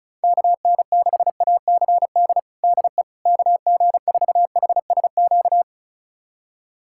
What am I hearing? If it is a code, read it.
Morse code, heard as a keyed tone, about 36 wpm, 700 Hz. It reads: KN6ACD DE KG4HSQ
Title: USA Callsign DE Callsign 35wpm